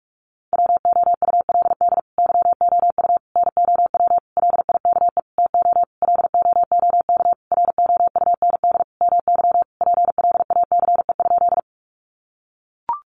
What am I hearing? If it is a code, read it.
Morse code, heard as a keyed tone, about 44 wpm, 700 Hz. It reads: WOULD YOU NOW LIKE TO LOOK ROUND MY PLACE?
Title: Sentences from Top 200 Words 45wpm